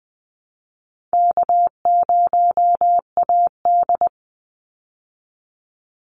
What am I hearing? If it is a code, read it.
Morse code, heard as a keyed tone, about 20 wpm, 700 Hz. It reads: K0AD
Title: Call Signs Repeated 3x - 20wpm